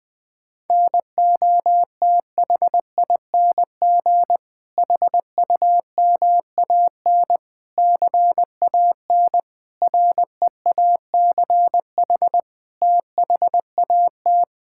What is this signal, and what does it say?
Morse code, heard as a keyed tone, about 20 wpm, 700 Hz. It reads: NOTHING HUMAN CAN REACH THAT
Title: Sentences from Top 1600 Words 20wpm